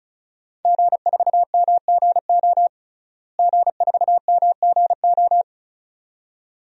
Morse code, heard as a keyed tone, about 36 wpm, 700 Hz. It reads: G4MGO G4MGO
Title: DXCC-100 Call Signs - 5 Characters - Repeated 2x - 35wpm